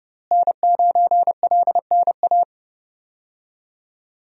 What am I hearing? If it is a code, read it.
Morse code, heard as a keyed tone, about 30 wpm, 700 Hz. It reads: N9LNA
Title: USA Call Signs - 5 Characters - Repeated 2x - 30wpm